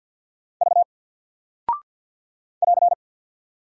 Morse code, heard as a keyed tone, about 48 wpm, 700 Hz. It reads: V <AR>
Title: Single Character (Letter:Number:Punctuation:Prosign) Alt 50wpm